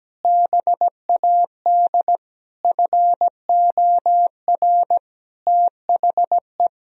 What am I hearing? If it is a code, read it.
Morse code, heard as a keyed tone, about 17 wpm, 700 Hz. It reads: BAD FOR THE
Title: Sentences from Top 1000 Words - Review Part 3 - 17wpm